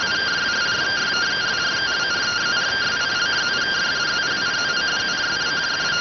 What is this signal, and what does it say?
Two-tone teleprinter signal, about 170 Hz shift.